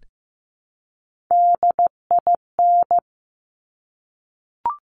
Morse code, heard as a keyed tone, about 15 wpm, 700 Hz. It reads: DIN
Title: Top Three Letter Combinations in Words 15wpm